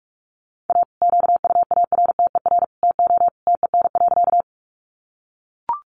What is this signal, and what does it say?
Morse code, heard as a keyed tone, about 44 wpm, 700 Hz. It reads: A QUARTER TO TEN.
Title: Sentences from Top 1300 Words 45wpm